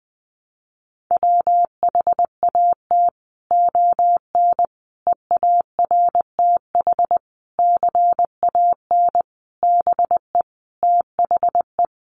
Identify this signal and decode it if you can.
Morse code, heard as a keyed tone, about 20 wpm, 700 Hz. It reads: WHAT ON EARTH CAN BE THE
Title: Sentences from Top 700 Words 20wpm